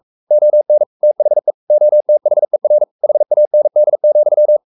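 Morse code, heard as a keyed tone, about 44 wpm, 600 Hz. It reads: ON THE OTHER HAND,
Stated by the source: Morse code at 43 wpm sending Ken Follett's novel Never in English, from CW med Gnister.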